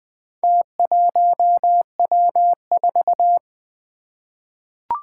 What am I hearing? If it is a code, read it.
Morse code, heard as a keyed tone, about 20 wpm, 700 Hz. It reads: T1W4
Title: Four Challenging Letter-Number Combinations 20wpm